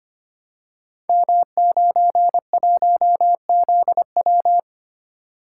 Morse code, heard as a keyed tone, about 25 wpm, 700 Hz. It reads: M91ZW
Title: Code Groups - 5 Letter:Numbers - 25wpm